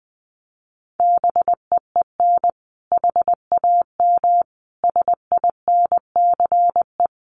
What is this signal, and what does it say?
Morse code, heard as a keyed tone, about 20 wpm, 700 Hz. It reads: BEEN HAM SINCE
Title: Common QSO Phrases 20wpm